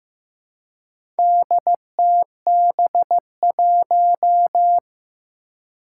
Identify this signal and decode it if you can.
Morse code, heard as a keyed tone, about 15 wpm, 700 Hz. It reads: DTB1
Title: Four Challenging Letter-Number Combinations 15wpm